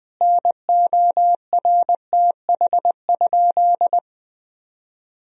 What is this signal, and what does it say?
Morse code, heard as a keyed tone, about 20 wpm, 700 Hz. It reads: NORTH?
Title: Sentences from Top 1000 Words - Review Part 2 - 20wpm